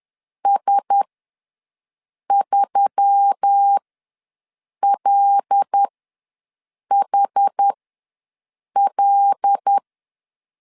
Morse code, heard as a keyed tone, about 10 wpm, spaced as for 6 wpm, 800 Hz. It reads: S3LHL
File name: LEZIONI / 5 LBVC / 5 L ws 10wpm 3636